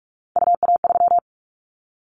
Morse code, heard as a keyed tone, about 44 wpm, 700 Hz. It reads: UA3
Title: Callsign Prefixes for Top 25 DXCC - Repeated 3x - 45wpm